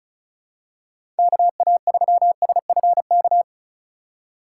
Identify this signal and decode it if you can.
Morse code, heard as a keyed tone, about 36 wpm, 700 Hz. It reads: KA3SFK
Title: USA Call Signs - 6 Characters - 4x Character Spacing - 35wpm